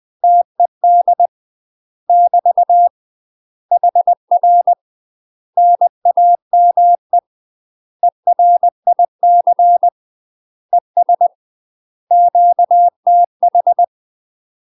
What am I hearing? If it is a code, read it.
Morse code, heard as a keyed tone, about 20 wpm, 700 Hz. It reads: TED <BT> HR NAME ERIC ES QTH
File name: UK MASTER QSOs TRAINING / 20WPM / UK MASTER QSOs 0022